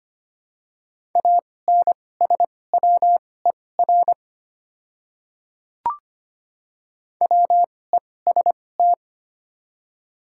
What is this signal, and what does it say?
Morse code, heard as a keyed tone, about 25 wpm, spaced as for 18 wpm, 700 Hz. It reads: ANSWER WEST
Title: 2x Character Spacing - Top 500 Words - 25wpm